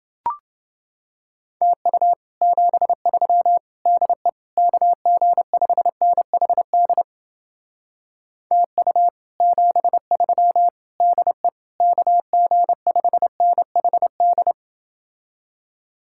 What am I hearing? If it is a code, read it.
Morse code, heard as a keyed tone, about 30 wpm, 700 Hz. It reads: TU 73 DE KG5NHD TU 73 DE KG5NHD